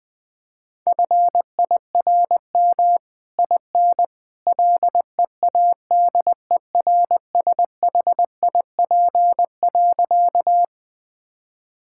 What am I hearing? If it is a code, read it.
Morse code, heard as a keyed tone, about 20 wpm, 700 Hz. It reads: FIRM IN LEADERSHIP.
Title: Sentences from Top 1600 Words 20wpm